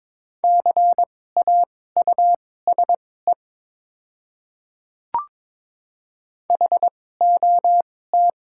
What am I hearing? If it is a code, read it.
Morse code, heard as a keyed tone, about 22 wpm, spaced as for 16 wpm, 700 Hz. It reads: CAUSE HOT
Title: 2x Character Spacing - Top 200 Words - 22wpm